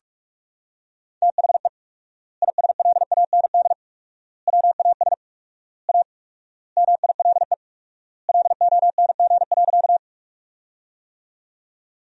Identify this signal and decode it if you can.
Morse code, heard as a keyed tone, about 41 wpm, 700 Hz. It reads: THE ISLAND WAS A MILE LONG.
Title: 4x Word-Spacing - Sentences from Top 500 Words - 45wpm